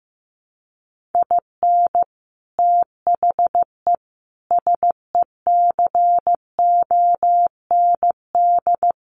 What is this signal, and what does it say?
Morse code, heard as a keyed tone, about 15 wpm, 700 Hz. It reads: IN THE SECOND
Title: Sets of 5 Words 15wpm